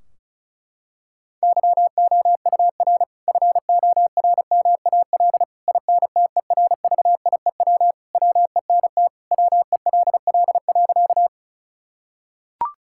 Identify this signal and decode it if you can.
Morse code, heard as a keyed tone, about 36 wpm, 700 Hz. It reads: YOUR FORMAL INTERVIEW WENT WELL.
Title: Sentences from Top 1600 Words 35wpm